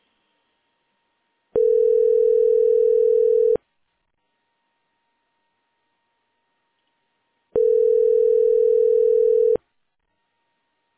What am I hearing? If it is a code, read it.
Ringback tone (North America), 440 + 480 Hz.